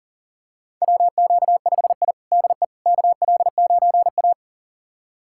Morse code, heard as a keyed tone, about 37 wpm, 700 Hz. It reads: WQ5I DE KL9A
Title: USA Callsign DE Callsign - 4 Characters - 40wpm